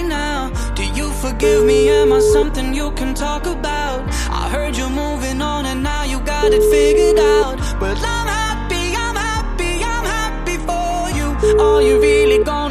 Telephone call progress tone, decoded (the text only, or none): ringback tone